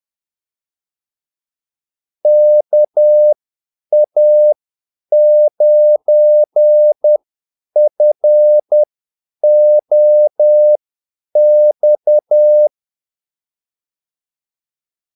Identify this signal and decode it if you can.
Morse code, heard as a keyed tone, about 10 wpm, spaced as for 8 wpm, 600 Hz. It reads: KA9FOX
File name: Callsigns-10-8wpm